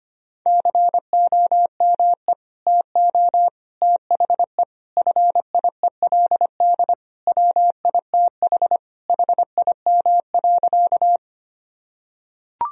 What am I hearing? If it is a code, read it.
Morse code, heard as a keyed tone, about 25 wpm, 700 Hz. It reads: COME TO THE FIELD WITH HIM.
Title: Sentences from Top 1000 Words - Review Part 2 - 25wpm